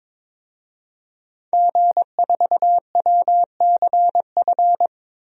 Morse code, heard as a keyed tone, about 22 wpm, 700 Hz. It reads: G4WCF
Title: DXCC-100 Call Signs - 5 Characters - Repeated 3x - 22wpm